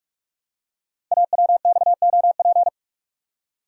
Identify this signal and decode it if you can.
Morse code, heard as a keyed tone, about 44 wpm, 700 Hz. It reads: AWXOP
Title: Code Groups - 5 Letters - 45wpm